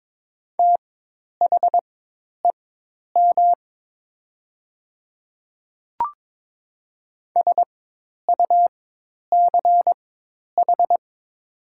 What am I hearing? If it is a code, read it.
Morse code, heard as a keyed tone, about 22 wpm, spaced as for 10 wpm, 700 Hz. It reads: THEM SUCH
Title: 4x Character Spacing - Top 300 Words - 22wpm